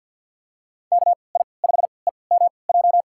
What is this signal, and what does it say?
Morse code, heard as a keyed tone, about 52 wpm, spaced as for 29 wpm, 700 Hz. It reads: KI5EMJ
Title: USA Call Signs - 6 Characters - 3x Character Spacing - 50wpm